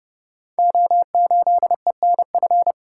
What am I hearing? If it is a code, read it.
Morse code, heard as a keyed tone, about 30 wpm, 700 Hz. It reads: O8ENF
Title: Code Groups - 5 Letter:Numbers - 30wpm